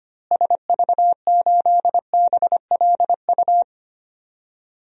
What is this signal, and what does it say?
Morse code, heard as a keyed tone, about 25 wpm, 700 Hz. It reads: SV8BLU
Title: DXCC-100 Call Signs - 6 Characters - Repeated 2x - 25wpm